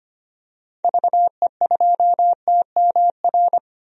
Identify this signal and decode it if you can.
Morse code, heard as a keyed tone, about 25 wpm, 700 Hz. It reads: VE2TMR
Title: DXCC-100 Call Signs - 6 Characters - Repeated 3x - 25wpm